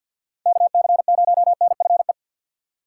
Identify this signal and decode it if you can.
Morse code, heard as a keyed tone, about 48 wpm, 700 Hz. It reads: KC0NFE
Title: USA Call Signs - 6 Characters - 4x Character Spacing - 50wpm